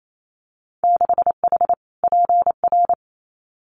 Morse code, heard as a keyed tone, about 28 wpm, 690 Hz. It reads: 6H PR
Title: Winter Field Day Exchanges Encore 28wpm